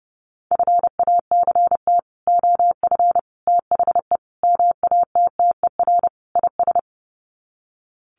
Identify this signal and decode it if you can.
Morse code, heard as a keyed tone, about 30 wpm, 700 Hz. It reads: FACT OF THE MATTER IS
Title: Sets of 5 Words 30wpm